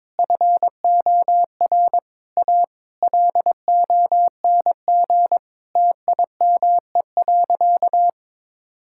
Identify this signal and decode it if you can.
Morse code, heard as a keyed tone, about 22 wpm, 700 Hz. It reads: FOR A LONG TIME.
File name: practice / 22wpm / Sentences with 4-Word Phrases - 22wpm